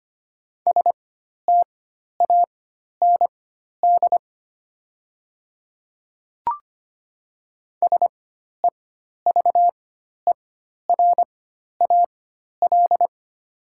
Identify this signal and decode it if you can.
Morse code, heard as a keyed tone, about 25 wpm, spaced as for 12 wpm, 700 Hz. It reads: STAND SEVERAL
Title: 4x Character Spacing - Top 500 Words - 25wpm